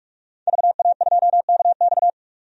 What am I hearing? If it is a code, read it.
Morse code, heard as a keyed tone, about 42 wpm, 700 Hz. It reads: VA2KX
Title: DXCC-100 Call Signs - 5 Characters - 45wpm